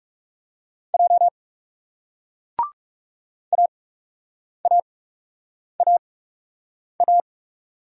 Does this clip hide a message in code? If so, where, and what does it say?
Yes: Morse code from 0.9 to 7.2 s, heard as a keyed tone, about 38 wpm, spaced as for 8 wpm, 700 Hz. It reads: J AAAA